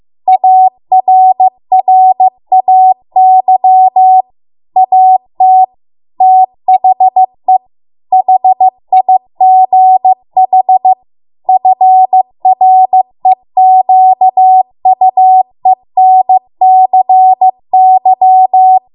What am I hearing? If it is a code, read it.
Morse code, heard as a keyed tone, about 15 wpm, 750 Hz. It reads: ARRAY AT THE HIGH FREQUENCY